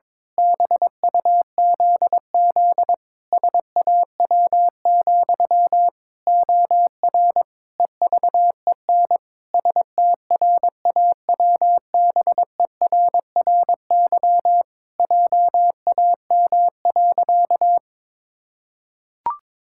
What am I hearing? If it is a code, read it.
Morse code, heard as a keyed tone, about 22 wpm, 700 Hz. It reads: BUZZ SAW, OR EVEN STRAWBERRY JAM.
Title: Pangram Challenge v2 - 22wpm